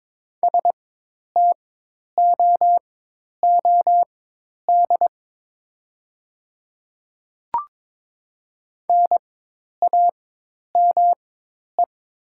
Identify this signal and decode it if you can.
Morse code, heard as a keyed tone, about 22 wpm, spaced as for 10 wpm, 700 Hz. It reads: STOOD NAME
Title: 4x Character Spacing - Top 500 Words - 22wpm